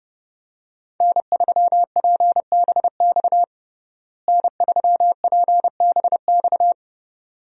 Morse code, heard as a keyed tone, about 30 wpm, 700 Hz. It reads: N3PBX N3PBX
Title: USA Call Signs - 5 Characters - Repeated 2x - 30wpm